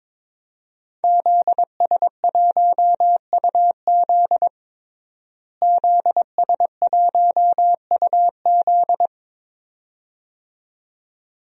Morse code, heard as a keyed tone, about 22 wpm, 700 Hz. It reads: ZS1UZ ZS1UZ
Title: DXCC-100 Call Signs - 5 Characters - Repeated 2x - 22wpm